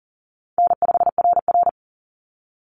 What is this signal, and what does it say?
Morse code, heard as a keyed tone, about 40 wpm, 700 Hz. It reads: N5RR
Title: Call Signs Repeated 2x - 40wpm